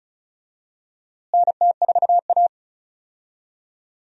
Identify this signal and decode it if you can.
Morse code, heard as a keyed tone, about 36 wpm, 700 Hz. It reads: NT4A
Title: USA Call Signs - 4 Characters - Repeated 2x - 35wpm